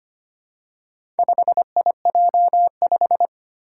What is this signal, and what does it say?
Morse code, heard as a keyed tone, about 25 wpm, 700 Hz. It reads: 5IJ5
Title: Four Challenging Letter-Number Combinations 25wpm